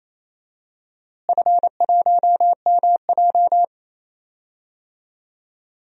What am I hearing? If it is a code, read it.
Morse code, heard as a keyed tone, about 28 wpm, 700 Hz. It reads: F1MJ
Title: DXCC-100 Call Signs - 4 Characters - Repeated 3x - 28wpm